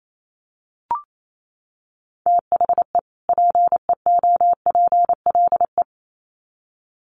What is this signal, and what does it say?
Morse code, heard as a keyed tone, about 28 wpm, 700 Hz. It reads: THE PEOPLE
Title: Sets of 2 Words 28wpm